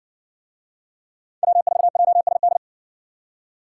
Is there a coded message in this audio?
Yes: Morse code from 1.4 to 2.6 s, heard as a keyed tone, about 60 wpm, 700 Hz. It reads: W4JSD